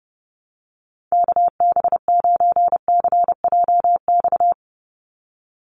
Morse code, heard as a keyed tone, about 30 wpm, 700 Hz. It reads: KB9CJX
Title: USA Call Signs 30wpm